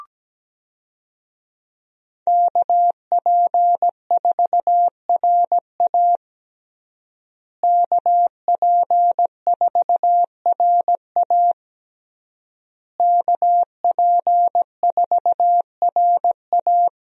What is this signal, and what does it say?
Morse code, heard as a keyed tone, about 17 wpm, 700 Hz. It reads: KP4RA KP4RA KP4RA
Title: DXCC-100 Call Signs - 5 Characters - Repeated 3x - 17wpm